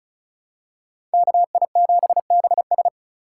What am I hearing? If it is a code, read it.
Morse code, heard as a keyed tone, about 34 wpm, 700 Hz. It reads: KI7BS
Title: USA Call Signs - 5 Characters - 2x Character Spacing - 35wpm